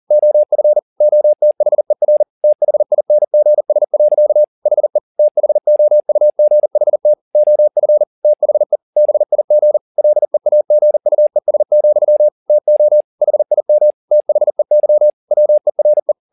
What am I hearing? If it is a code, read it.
Morse code, heard as a keyed tone, about 40 wpm, 600 Hz. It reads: OF OTHER THINGS. HE THOUGHT OF THE BIG LEAGUES, TO HIM THEY WERE